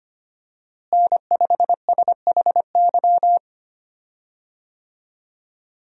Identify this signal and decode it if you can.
Morse code, heard as a keyed tone, about 25 wpm, 700 Hz. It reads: N5SHY
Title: USA Call Signs - 5 Characters - Repeated 2x - 25wpm